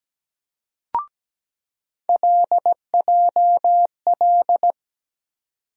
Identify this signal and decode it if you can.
Morse code, heard as a keyed tone, about 17 wpm, 700 Hz. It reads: LJL
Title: USA Call Sign Postfixes - 3 Letters 17wpm